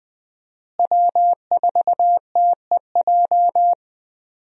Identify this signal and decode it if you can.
Morse code, heard as a keyed tone, about 21 wpm, 700 Hz. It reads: W4TEJ
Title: USA Call Signs - 5 Characters - Repeated 3x - 20wpm